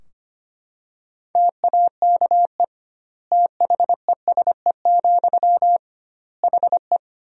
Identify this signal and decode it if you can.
Morse code, heard as a keyed tone, about 25 wpm, 700 Hz. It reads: TAKE THESE, HE
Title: 2x Word-Spacing - Sentences from Top 300 Words - 25wpm